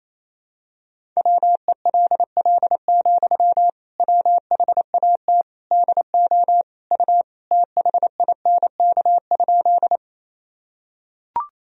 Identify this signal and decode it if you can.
Morse code, heard as a keyed tone, about 28 wpm, 700 Hz. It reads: WELL, WHAT DO U THINK?
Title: Sentences from Top 200 Words and Abbreviations 28wpm